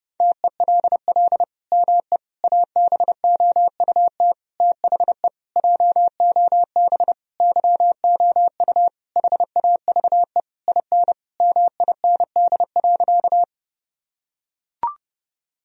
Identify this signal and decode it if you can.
Morse code, heard as a keyed tone, about 30 wpm, 700 Hz. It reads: TELL ME ABOUT THE JOB YOU HAVE IN MIND.
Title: Sentences from Top 1000 Words - Review Part 3 - 30wpm